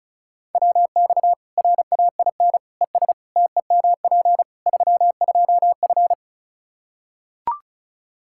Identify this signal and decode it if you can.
Morse code, heard as a keyed tone, about 34 wpm, 700 Hz. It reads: WX RAIN ES TEMP 32F
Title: K1USN SST QSO Practice 35wpm